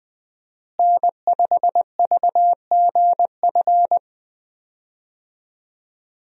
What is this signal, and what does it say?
Morse code, heard as a keyed tone, about 19 wpm, 700 Hz. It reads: N5VGF
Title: USA Call Signs - 5 Characters - Repeated 3x - 20wpm